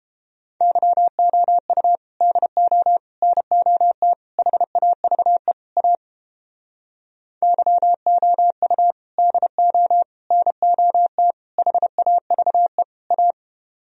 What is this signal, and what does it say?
Morse code, heard as a keyed tone, about 29 wpm, 700 Hz. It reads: YOU DO NOT HAVE A YOU DO NOT HAVE A